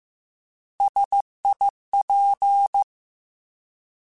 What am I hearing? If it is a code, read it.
Morse code, heard as a keyed tone, about 15 wpm, 800 Hz. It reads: SIP
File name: Alphabetical / Alphabetical mp3s / Ss @13